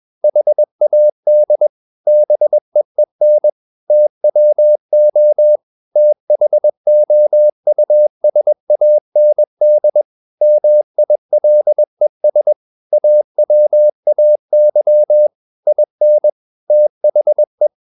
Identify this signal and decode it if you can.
Morse code, heard as a keyed tone, about 21 wpm, 600 Hz. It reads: HAD BEEN TWO THOUSAND MILES AWAY IN THE
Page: CW med Gnister